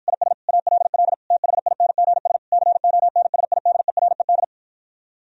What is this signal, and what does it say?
Morse code, heard as a keyed tone, about 52 wpm, 700 Hz. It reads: IS ALL THINGS CONSIDERED